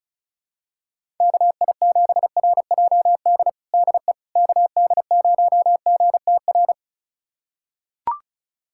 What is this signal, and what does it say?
Morse code, heard as a keyed tone, about 36 wpm, 700 Hz. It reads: KI7RJD DE KD0GTR
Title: USA Callsign DE Callsign - 6 Characters - 35wpm